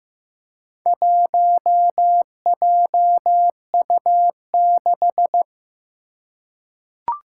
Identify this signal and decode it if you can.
Morse code, heard as a keyed tone, about 15 wpm, 700 Hz. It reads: 1JU6
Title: Four Challenging Letter-Number Combinations 15wpm